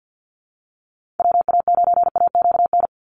Morse code, heard as a keyed tone, about 48 wpm, 700 Hz. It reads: WA9AQN
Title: Call Signs - 50wpm